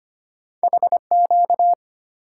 Morse code, heard as a keyed tone, about 25 wpm, 700 Hz. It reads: HQ
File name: Two Letter-Number Combinations - 25wpm